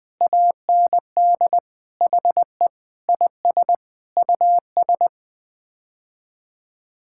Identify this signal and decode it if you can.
Morse code, heard as a keyed tone, about 20 wpm, 700 Hz. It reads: AND HE IS US
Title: Sets of 9 Words - 20wpm